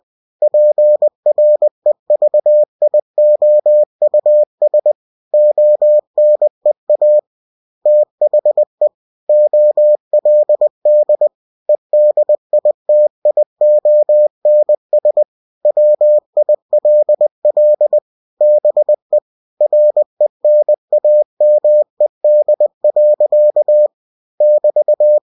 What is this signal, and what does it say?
Morse code, heard as a keyed tone, about 20 wpm, 600 Hz. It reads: PREVIOUS ONEA THE OLD EDITIONS WILL BE RENAMED. <BT>
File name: cw-books / winnie-the-pooh / winnie-20wpm-page-0112